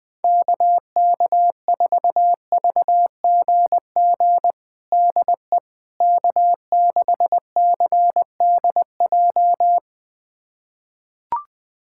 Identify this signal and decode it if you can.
Morse code, heard as a keyed tone, about 20 wpm, 700 Hz. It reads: KK4VGG DE K6CDJ
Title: USA Callsign DE Callsign 20wpm